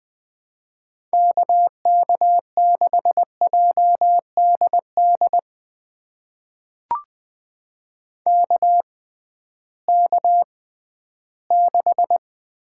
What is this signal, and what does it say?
Morse code, heard as a keyed tone, about 20 wpm, spaced as for 13 wpm, 700 Hz. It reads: KK6JDD KK6
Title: USA Call Signs - 6 Characters - 6x Character Spacing - 20wpm